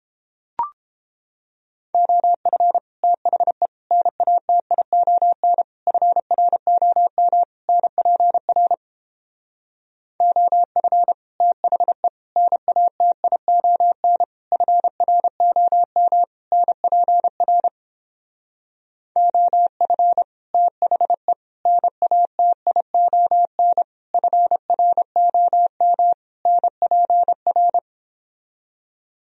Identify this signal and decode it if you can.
Morse code, heard as a keyed tone, about 29 wpm, 700 Hz. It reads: OF THE NATION FROM NPR OF THE NATION FROM NPR OF THE NATION FROM NPR